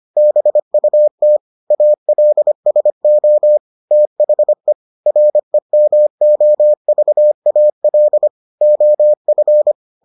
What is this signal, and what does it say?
Morse code, heard as a keyed tone, about 25 wpm, 600 Hz. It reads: BUT ALSO THE REMOVAL OF